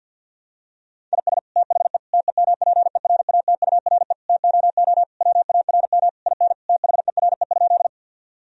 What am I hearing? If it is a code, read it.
Morse code, heard as a keyed tone, about 50 wpm, 700 Hz. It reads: IS THE TEMPERATURE TOO WARM IN THERE?
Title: Sentences from Top 1000 Words - Review Part 4 - 50wpm